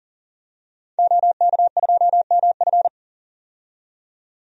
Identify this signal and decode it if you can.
Morse code, heard as a keyed tone, about 40 wpm, 700 Hz. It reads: OK2MF
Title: DXCC-100 Call Signs - 5 Characters - 40wpm